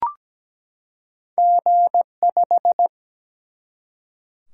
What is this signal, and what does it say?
Morse code, heard as a keyed tone, about 17 wpm, 700 Hz. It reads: G5
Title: Single-Letter Single-Number 17wpm